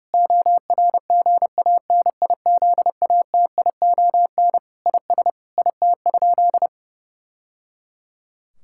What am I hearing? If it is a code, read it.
Morse code, heard as a keyed tone, about 30 wpm, 700 Hz. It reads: ORGANIZATION IS IT?
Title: Sentences from Top 1000 Words - Review Part 5 - 30wpm